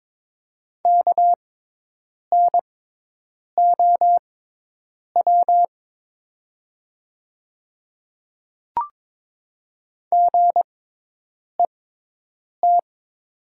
Morse code, heard as a keyed tone, about 22 wpm, spaced as for 8 wpm, 700 Hz. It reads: KNOW GET
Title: 6x Character Spacing - Top 200 Words - 22wpm